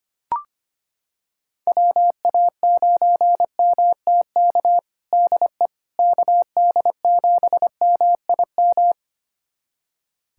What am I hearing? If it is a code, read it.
Morse code, heard as a keyed tone, about 25 wpm, 700 Hz. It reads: WA9MTK DE KD7MIM
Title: USA Callsign DE Callsign - 6 Characters - 25wpm